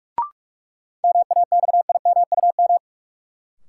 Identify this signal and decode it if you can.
Morse code, heard as a keyed tone, about 41 wpm, 700 Hz. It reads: MAXIMUM
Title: Sentences from Top 1300 Words - New Word Drill Alt - 45wpm